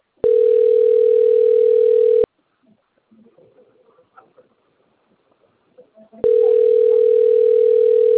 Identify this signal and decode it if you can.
Ringback tone (North America), 440 + 480 Hz.